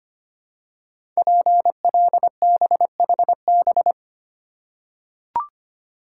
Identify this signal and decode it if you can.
Morse code, heard as a keyed tone, about 25 wpm, 700 Hz. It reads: PLBHB
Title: Code Groups - 5 Letter:Numbers - 25wpm